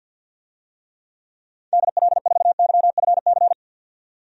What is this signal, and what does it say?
Morse code, heard as a keyed tone, about 52 wpm, 700 Hz. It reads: DL4YFC
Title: DXCC-100 Call Signs - 6 Characters - 50wpm